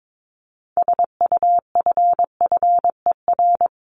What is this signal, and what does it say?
Morse code, heard as a keyed tone, about 22 wpm, 700 Hz. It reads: SUFFER
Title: Top 1501-1750 Words 22wpm